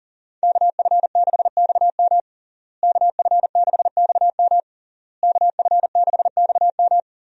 Morse code, heard as a keyed tone, about 40 wpm, 700 Hz. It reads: KF6XM KF6XM KF6XM
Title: USA Call Signs - 5 Characters - Repeated 3x - 40wpm